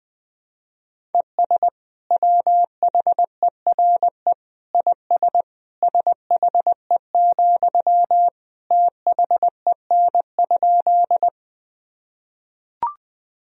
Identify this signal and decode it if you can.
Morse code, heard as a keyed tone, about 20 wpm, 700 Hz. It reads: ES WHERE IS SHE, THEN?
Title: Sentences from Top 200 Words and Abbreviations 20wpm